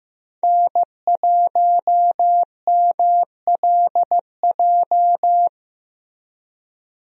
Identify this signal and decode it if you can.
Morse code, heard as a keyed tone, about 15 wpm, 700 Hz. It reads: N1MLJ
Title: USA Call Signs - 5 Characters - Repeated 3x - 15wpm